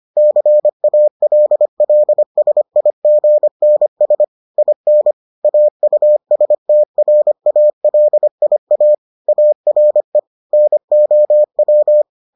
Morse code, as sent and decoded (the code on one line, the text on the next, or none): -.-. .- .-.. .-.. ... .. --. -. ... / .. -. / .- ..- ... - .-. .- .-.. .. .- / .- .-. / -. --- .--
CALLSIGNS IN AUSTRALIA ARE NOW